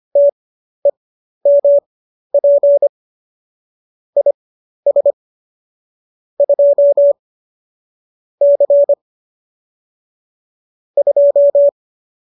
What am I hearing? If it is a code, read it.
Morse code, heard as a keyed tone, about 25 wpm, spaced as for 12 wpm, 570 Hz. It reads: TEMP IS 2 C 2
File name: qso012 12